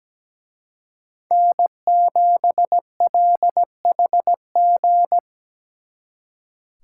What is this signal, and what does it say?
Morse code, heard as a keyed tone, about 20 wpm, 700 Hz. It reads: N7LHG
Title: Speed Racing - USA 1x3 Call Signs 17wpm